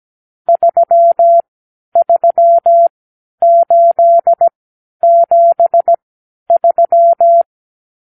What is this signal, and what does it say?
Morse code, heard as a keyed tone, about 17 wpm, spaced as for 10 wpm, 680 Hz. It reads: 33873
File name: arrl / Review (1-0)